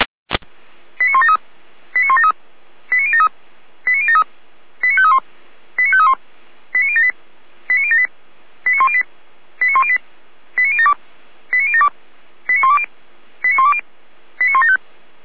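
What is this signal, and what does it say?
Five-tone selective call (ZVEI), heard as quick runs of five tones, tones of about 70 ms. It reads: 89173 89173 89073 89073 89531 89531 89078 89078 89108 89108 89072 89072 89120 89120 89175
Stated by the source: ZVEI five-tone selective call 89022 received on SDR by feuerwehr-orlamuende (fire brigade).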